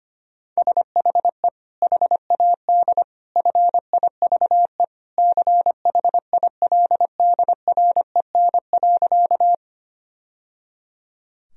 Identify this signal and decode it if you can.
Morse code, heard as a keyed tone, about 25 wpm, 700 Hz. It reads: SHE HAD FIVE CHILDREN.
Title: Sentences from Top 1000 Words - Review Part 2 - 25wpm